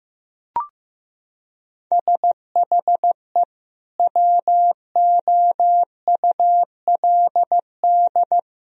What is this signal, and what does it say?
Morse code, heard as a keyed tone, about 16 wpm, 700 Hz. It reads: SHE WOULD
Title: Sentences from Top 1000 Words - Review Part 3 - 15wpm